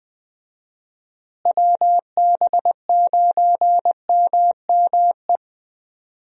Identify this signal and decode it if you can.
Morse code, heard as a keyed tone, about 21 wpm, 700 Hz. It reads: WB9MME
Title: USA Call Signs - 6 Characters - Repeated 3x - 20wpm